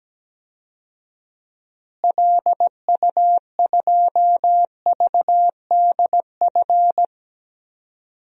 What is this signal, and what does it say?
Morse code, heard as a keyed tone, about 17 wpm, 700 Hz. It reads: LU2VDF
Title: DXCC-100 Call Signs - 6 Characters - Repeated 3x - 17wpm